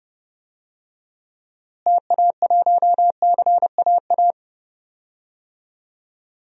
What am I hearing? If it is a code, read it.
Morse code, heard as a keyed tone, about 31 wpm, 700 Hz. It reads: TA1CAA
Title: DXCC-100 Call Signs - 6 Characters - Repeated 2x - 30wpm